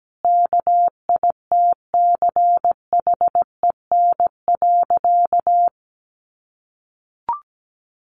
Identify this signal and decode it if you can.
Morse code, heard as a keyed tone, about 17 wpm, 700 Hz. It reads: KITCHEN.
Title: Sentences from Top 1400 Words 17wpm